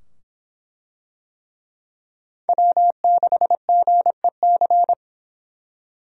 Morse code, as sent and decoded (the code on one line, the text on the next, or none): .-- -.... --. -.-.
W6GEC